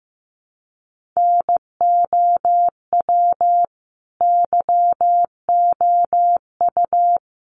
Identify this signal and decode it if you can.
Morse code, heard as a keyed tone, about 15 wpm, 700 Hz. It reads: NOW YOU
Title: Sentences from Top 100 Words 15wpm